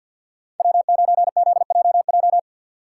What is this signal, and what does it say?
Morse code, heard as a keyed tone, about 52 wpm, 700 Hz. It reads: W9ZJJ
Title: USA Call Signs - 5 Characters - 6x Character Spacing - 50wpm